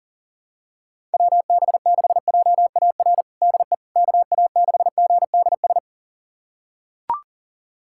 Morse code, heard as a keyed tone, about 40 wpm, 700 Hz. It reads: WB6JAR DE KA6GDS
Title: USA Callsign DE Callsign - 6 Characters - 40wpm